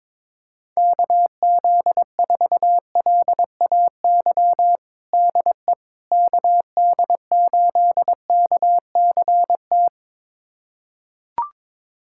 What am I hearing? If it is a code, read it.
Morse code, heard as a keyed tone, about 22 wpm, 700 Hz. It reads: KZ4LAY DE KD8KCT
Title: USA Callsign DE Callsign - 6 Characters - 22wpm